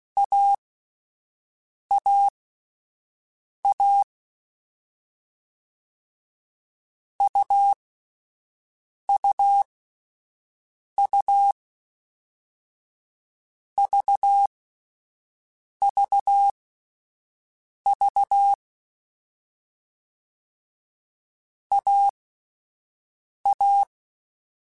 Morse code, heard as a keyed tone, about 16 wpm, spaced as for 5 wpm, 800 Hz. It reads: AAA UUU VVV AA